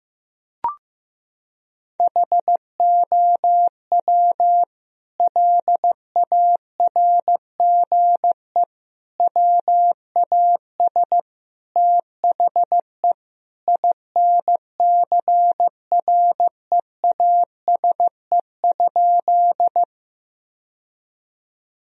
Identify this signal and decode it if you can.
Morse code, heard as a keyed tone, about 15 wpm, 700 Hz. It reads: HOW LARGE WAS THE INCREASE?
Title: Sentences from Top 1000 Words - Review Part 4 - 15wpm